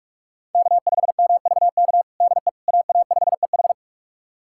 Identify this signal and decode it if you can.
Morse code, heard as a keyed tone, about 41 wpm, 700 Hz. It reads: K5MVK DE AA5EH
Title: USA Callsign DE Callsign - 5 Characters - 45wpm